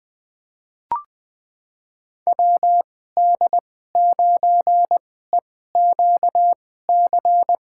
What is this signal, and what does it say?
Morse code, heard as a keyed tone, about 20 wpm, spaced as for 14 wpm, 700 Hz. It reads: WD9EQC
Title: USA Call Signs - 6 Characters - 2x Character Spacing - 20wpm